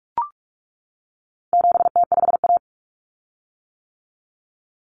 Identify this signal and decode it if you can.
Morse code, heard as a keyed tone, about 44 wpm, 700 Hz. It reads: 7T5A
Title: Common Cut Numbers - 4-Digit 45wpm